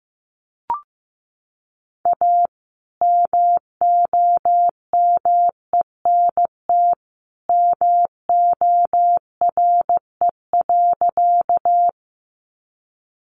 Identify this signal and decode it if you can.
Morse code, heard as a keyed tone, about 15 wpm, 700 Hz. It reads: A MOMENT MORE.
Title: Sentences from Top 1000 Words 15wpm